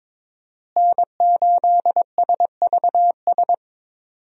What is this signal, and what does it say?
Morse code, heard as a keyed tone, about 21 wpm, 700 Hz. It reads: N8SVS
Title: USA Call Signs - 5 Characters - Repeated 2x - 22wpm